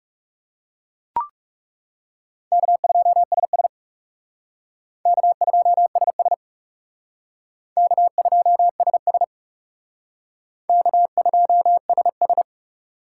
Morse code, heard as a keyed tone, about 38 wpm, 700 Hz. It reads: K2SS K2SS K2SS K2SS